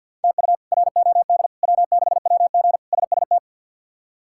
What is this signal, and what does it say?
Morse code, heard as a keyed tone, about 47 wpm, 700 Hz. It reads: TU ROD W6WG SST
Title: K1USN SST QSO Practice - 50wpm